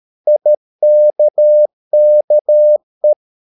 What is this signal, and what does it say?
Morse code, heard as a keyed tone, about 13 wpm, 600 Hz. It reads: IKKE
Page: Ferie 13 wpm